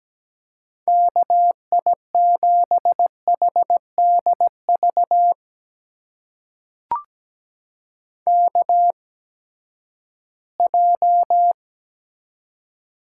Morse code, heard as a keyed tone, about 17 wpm, spaced as for 12 wpm, 700 Hz. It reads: KI7HDV KJ
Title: USA Call Signs - 6 Characters - 8x Character Spacing - 17wpm